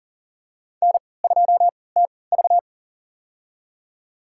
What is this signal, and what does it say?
Morse code, heard as a keyed tone, about 38 wpm, spaced as for 22 wpm, 700 Hz. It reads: N2TV
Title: USA Call Signs - 4 Characters - 3x Character Spacing - 40wpm